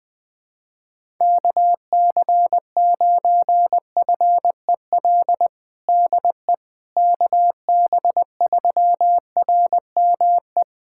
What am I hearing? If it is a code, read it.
Morse code, heard as a keyed tone, about 20 wpm, 700 Hz. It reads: KC9FEL DE KB3RME
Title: USA Callsign DE Callsign - 6 Characters - 20wpm